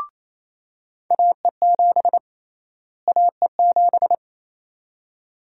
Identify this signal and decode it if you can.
Morse code, heard as a keyed tone, about 28 wpm, 700 Hz. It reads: AE7 AE7
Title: USA Call Sign Prefixes - Repeated 2x - 28wpm